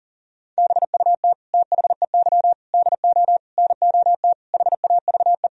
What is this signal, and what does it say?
Morse code, heard as a keyed tone, about 40 wpm, 700 Hz. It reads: BUT THEY DO NOT HAVE
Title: Sentences from Top 1000 Words - Review Part 4 - 40wpm